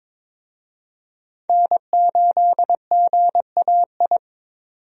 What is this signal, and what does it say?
Morse code, heard as a keyed tone, about 22 wpm, 700 Hz. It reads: N8GAI
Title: USA Call Signs - 5 Characters - 22wpm